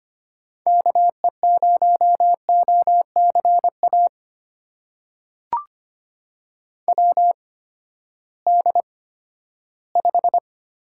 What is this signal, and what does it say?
Morse code, heard as a keyed tone, about 25 wpm, spaced as for 14 wpm, 700 Hz. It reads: KE0OCA WD5
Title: USA Call Signs - 6 Characters - 8x Character Spacing - 25wpm